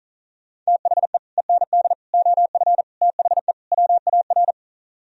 Morse code, heard as a keyed tone, about 42 wpm, 700 Hz. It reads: THE END OF THE WAR
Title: Speed-Racing - Sets of 5 Words 35wpm